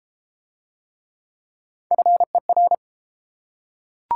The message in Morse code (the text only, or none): FER